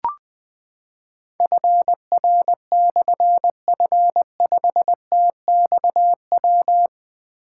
Morse code, heard as a keyed tone, about 20 wpm, 700 Hz. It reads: FR/F5TXW
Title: Difficult Call Signs 20wpm